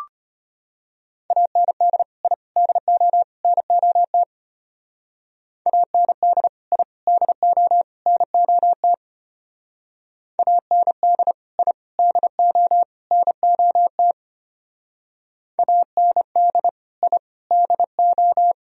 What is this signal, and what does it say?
Morse code, heard as a keyed tone, about 30 wpm, 700 Hz. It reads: AND I DO NOT AND I DO NOT AND I DO NOT AND I DO
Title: Speed-Racing - Sets of 4 Words 25wpm